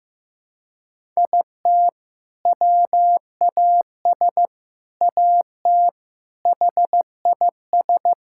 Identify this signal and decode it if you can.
Morse code, heard as a keyed tone, about 15 wpm, 700 Hz. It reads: IT WAS AT HIS
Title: Sentences from Top 1600 Words 15wpm